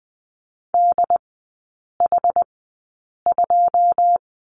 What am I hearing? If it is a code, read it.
Morse code, heard as a keyed tone, about 20 wpm, spaced as for 8 wpm, 700 Hz. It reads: DH2
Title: 107 Lesson 18e - Letter m - Three Characters 20wpm